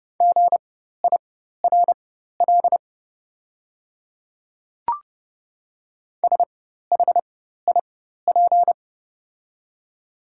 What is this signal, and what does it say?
Morse code, heard as a keyed tone, about 30 wpm, spaced as for 14 wpm, 690 Hz. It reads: GIRL SHIP
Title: 4x Character Spacing - Top 500 Words - 30wpm